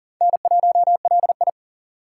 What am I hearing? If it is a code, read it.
Morse code, heard as a keyed tone, about 40 wpm, 700 Hz. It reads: N1LI
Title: USA Call Signs - 4 Characters - 3x Character Spacing - 40wpm